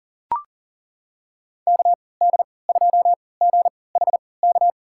Morse code, heard as a keyed tone, about 40 wpm, spaced as for 23 wpm, 700 Hz. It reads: KD2GHK
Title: USA Call Signs - 3x Character Spacing - 40wpm